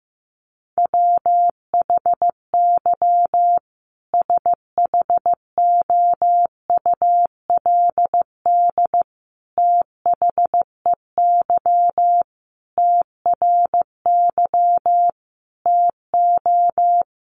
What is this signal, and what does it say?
Morse code, heard as a keyed tone, about 15 wpm, 700 Hz. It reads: WHY SHOULD THEY TRY TO
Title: Sentences from Top 1100 Words 15wpm